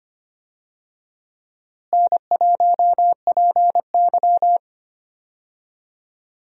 Morse code, heard as a keyed tone, about 25 wpm, 700 Hz. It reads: N1PY
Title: USA Call Signs - 4 Characters - Repeated 2x - 25wpm